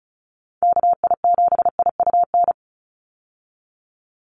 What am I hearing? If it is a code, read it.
Morse code, heard as a keyed tone, about 36 wpm, 700 Hz. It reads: KI7IUN
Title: USA Call Signs 35wpm